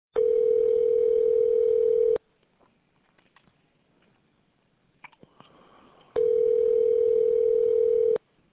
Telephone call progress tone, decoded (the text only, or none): ringback tone